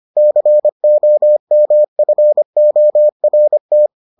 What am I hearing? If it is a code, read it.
Morse code, heard as a keyed tone, about 25 wpm, 600 Hz. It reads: COMFORT